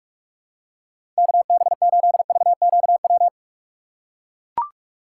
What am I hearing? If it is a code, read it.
Morse code, heard as a keyed tone, about 42 wpm, 700 Hz. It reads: KB8VQW
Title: USA Call Signs - 8x Character Spacing - 45wpm